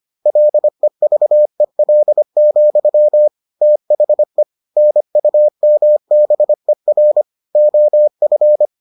Morse code, heard as a keyed tone, about 25 wpm, 600 Hz. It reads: LEVEL, THE NUMBER OF